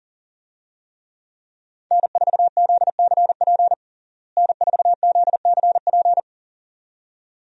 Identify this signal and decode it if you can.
Morse code, heard as a keyed tone, about 37 wpm, 700 Hz. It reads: N4ZCP N4ZCP
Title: USA Call Signs - 5 Characters - Repeated 2x - 40wpm